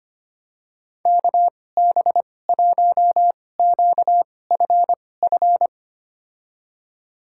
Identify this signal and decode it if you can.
Morse code, heard as a keyed tone, about 25 wpm, spaced as for 18 wpm, 700 Hz. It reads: KB1QFF
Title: USA Call Signs - 6 Characters - 2x Character Spacing - 25wpm